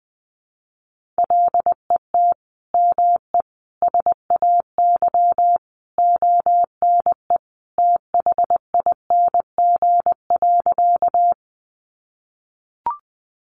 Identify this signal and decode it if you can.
Morse code, heard as a keyed tone, about 20 wpm, 700 Hz. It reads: LET ME SAY ONE THING.
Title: Sentences from Top 300 Words 20wpm